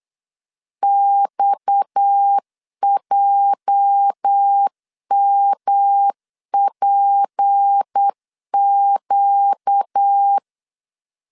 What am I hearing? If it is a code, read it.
Morse code, heard as a keyed tone, about 8 wpm, 800 Hz. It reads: XJMPQ
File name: LEZIONI / 8 YJ45 / 8 YJ4 8wpm 3246